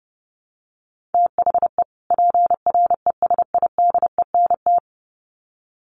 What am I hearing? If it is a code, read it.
Morse code, heard as a keyed tone, about 30 wpm, 700 Hz. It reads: THE PRESIDENT
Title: Sets of 2 Words 30wpm